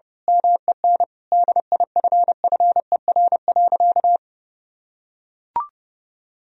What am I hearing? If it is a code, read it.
Morse code, heard as a keyed tone, about 32 wpm, 700 Hz. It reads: MEN DIFFER.
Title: Sentences from Top 1000 Words - Review Part 1 - 30wpm